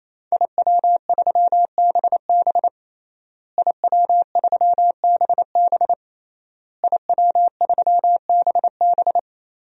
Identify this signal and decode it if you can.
Morse code, heard as a keyed tone, about 28 wpm, 700 Hz. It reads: IW3BB IW3BB IW3BB
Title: DXCC-100 Call Signs - 5 Characters - Repeated 3x - 28wpm